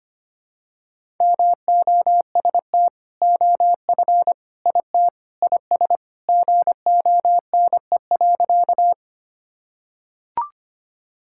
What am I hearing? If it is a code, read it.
Morse code, heard as a keyed tone, about 25 wpm, 700 Hz. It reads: MOST OF IT IS GONE.
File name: practice / 25wpm / Sentences with 2-Word Phrases - 25wpm